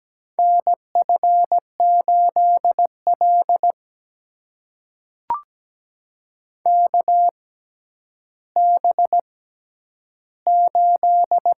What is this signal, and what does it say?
Morse code, heard as a keyed tone, about 17 wpm, spaced as for 10 wpm, 700 Hz. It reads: NF8L KB8
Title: USA Call Signs - 4 Characters - 6x Character Spacing - 17wpm